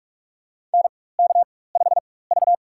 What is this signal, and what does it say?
Morse code, heard as a keyed tone, about 46 wpm, spaced as for 21 wpm, 700 Hz. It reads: NK5V
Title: USA Call Signs - 4 Characters - 4x Character Spacing - 45wpm